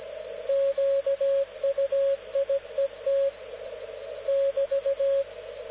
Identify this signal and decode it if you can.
Morse code, heard as a keyed tone, about 17 wpm, 550 Hz, with radio noise. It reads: QUIET <BT>
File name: luxorion / Radio / dk0wcy-spaceweather-10144khz